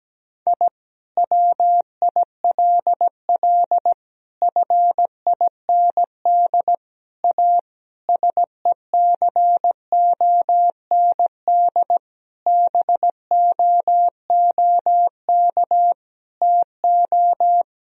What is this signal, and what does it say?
Morse code, heard as a keyed tone, about 17 wpm, 700 Hz. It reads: I WILL FIND A SECOND BOOK TO
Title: Sentences from Top 1000 Words - Review Part 1 - 17wpm